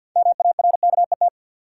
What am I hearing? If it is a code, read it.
Morse code, heard as a keyed tone, about 51 wpm, 700 Hz. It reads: MARKET
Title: Sentences from Top 600 Words - New Word Drill - 50wpm